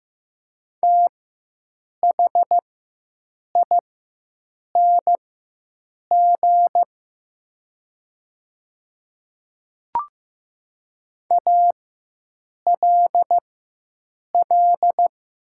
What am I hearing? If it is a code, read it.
Morse code, heard as a keyed tone, about 15 wpm, spaced as for 7 wpm, 700 Hz. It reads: THING ALL
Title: 4x Character Spacing - Top 200 Words - 15wpm